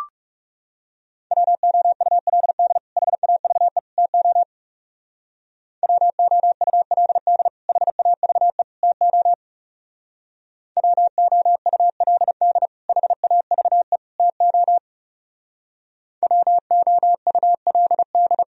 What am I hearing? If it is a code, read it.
Morse code, heard as a keyed tone, about 37 wpm, 700 Hz. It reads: WOULD HAVE TO WOULD HAVE TO WOULD HAVE TO WOULD
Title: Speed-Racing - Sets of 3 Words 30wpm